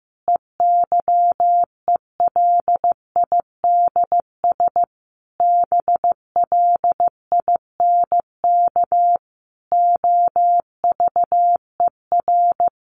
Morse code, heard as a keyed tone, about 15 wpm, 700 Hz. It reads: EYELIDS BLINK OVER
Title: A Sound Of Thunder 15wpm